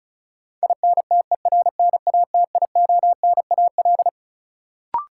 Morse code, heard as a keyed tone, about 36 wpm, 700 Hz. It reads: INTERNATIONAL
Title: Sentences from Top 1000 Words - Review Part 4 - New Word Drill - 35wpm